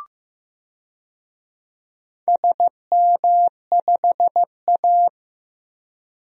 Morse code, heard as a keyed tone, about 15 wpm, 700 Hz. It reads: SM5A
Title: DXCC-100 Call Signs - 4 Characters - 15wpm